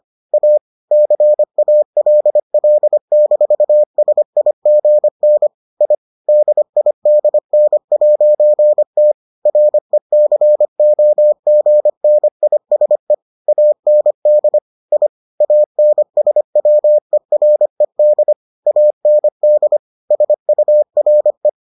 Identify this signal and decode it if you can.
Morse code, heard as a keyed tone, about 25 wpm, 600 Hz. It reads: A CALL-SIGN I DIDN'T RECOGNISE AND I ANSWERED AND SURE